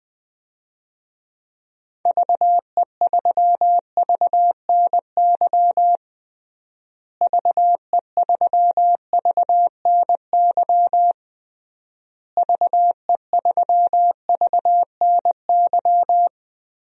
Morse code, heard as a keyed tone, about 20 wpm, 700 Hz. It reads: VE3VNY VE3VNY VE3VNY
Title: DXCC-100 Call Signs - 6 Characters - Repeated 3x - 20wpm